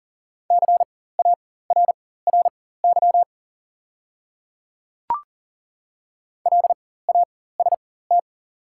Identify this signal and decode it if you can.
Morse code, heard as a keyed tone, about 40 wpm, spaced as for 19 wpm, 690 Hz. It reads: CARRY LAST
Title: 4x Character Spacing - Top 300 Words - 40wpm